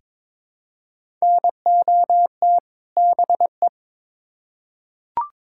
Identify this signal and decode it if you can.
Morse code, heard as a keyed tone, about 22 wpm, 700 Hz. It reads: NOT BE
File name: Build A Phrase - 2-Word Phrases - 22wpm